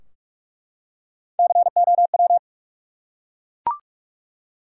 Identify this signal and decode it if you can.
Morse code, heard as a keyed tone, about 48 wpm, 700 Hz. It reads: COW
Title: Three Letter Words - 45wpm